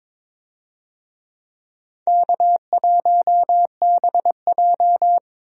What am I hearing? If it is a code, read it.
Morse code, heard as a keyed tone, about 22 wpm, 700 Hz. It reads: K1BJ
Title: USA Call Signs - 4 Characters - 22wpm